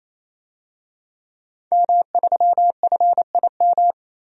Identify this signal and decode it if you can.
Morse code, heard as a keyed tone, about 28 wpm, 700 Hz. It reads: M3FIM
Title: DXCC-100 Call Signs - 5 Characters - 28wpm